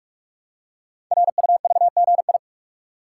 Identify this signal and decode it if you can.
Morse code, heard as a keyed tone, about 44 wpm, 700 Hz. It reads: RUVGI
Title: Code Groups - 5 Letter:Numbers - 45wpm